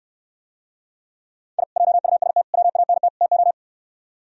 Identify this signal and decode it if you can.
Morse code, heard as a keyed tone, about 67 wpm, 700 Hz. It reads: I JUST WANT TO